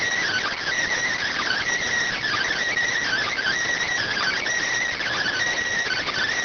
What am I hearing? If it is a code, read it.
Emergency Alert System (SAME) tones (header bursts), heard as harsh buzzing bursts. It is heard from 1.0 to 5.4 s.